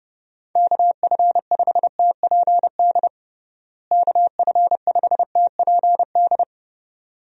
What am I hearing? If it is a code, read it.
Morse code, heard as a keyed tone, about 30 wpm, 700 Hz. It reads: KF5TPD KF5TPD
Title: USA Call Signs - 6 Characters - Repeated 2x - 30wpm